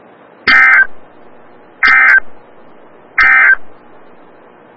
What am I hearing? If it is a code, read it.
Emergency Alert System (SAME) tones, heard as harsh buzzing bursts. It is heard from 0.4 to 3.6 s.
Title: Des Moines, IA NOAA Weather Radio - WXL57